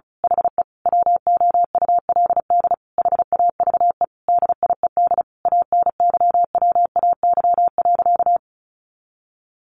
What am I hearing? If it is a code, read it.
Morse code, heard as a keyed tone, about 34 wpm, 700 Hz. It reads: HE WOULD HAVE DIED ANYWAY.
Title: A Sound Of Thunder 35wpm